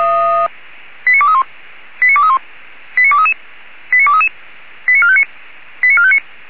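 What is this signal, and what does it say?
Five-tone selective call (ZVEI), heard as quick runs of five tones, tones of about 70 ms. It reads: 89231 89231 89233 89233 89459 89459
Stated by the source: Five-tone ZVEI selective call received by SDR.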